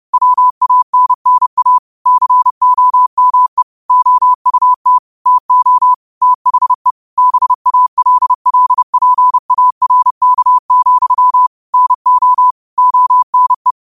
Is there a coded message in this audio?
Yes: Morse code from 0.1 to 13.7 s, heard as a keyed tone, about 30 wpm, 1000 Hz. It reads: WANNA COME OUT TO THE BALLPARK, NO ONE